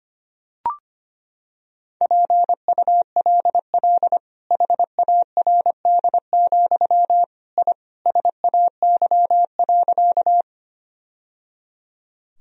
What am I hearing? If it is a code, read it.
Morse code, heard as a keyed tone, about 25 wpm, 700 Hz. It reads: PULL HARD, I SAY.
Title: Sentences from Top 1000 Words - Review Part 2 - 25wpm